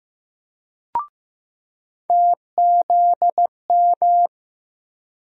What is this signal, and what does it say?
Morse code, heard as a keyed tone, about 15 wpm, 700 Hz. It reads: TZM
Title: USA Call Sign Postfixes - 3 Letters 15wpm